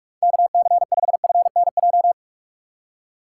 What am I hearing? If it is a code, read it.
Morse code, heard as a keyed tone, about 46 wpm, 700 Hz. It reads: KC5FNJ
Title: USA Call Signs - 6 Characters - 4x Character Spacing - 45wpm